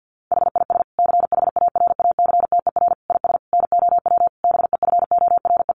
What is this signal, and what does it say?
Morse code, heard as a keyed tone, about 50 wpm, 700 Hz. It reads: HIS CHARACTER IS NOW BEFORE
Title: Sentences from Top 1200 Words 50wpm